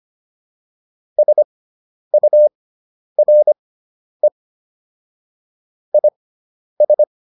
Morse code, heard as a keyed tone, about 25 wpm, spaced as for 10 wpm, 600 Hz. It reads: SURE IS